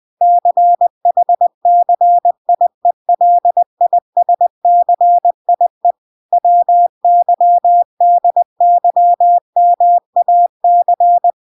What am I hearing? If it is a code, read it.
Morse code, heard as a keyed tone, about 20 wpm, 700 Hz. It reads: CHCIELISCIE WYDYMAC